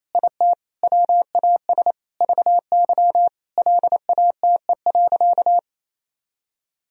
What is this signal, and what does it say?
Morse code, heard as a keyed tone, about 28 wpm, 700 Hz. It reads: IT WAS VY LATE.
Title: Sentences from Top 300 Words and Abbreviations 28wpm